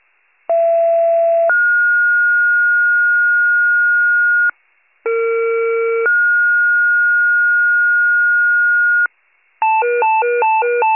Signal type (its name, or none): two-tone page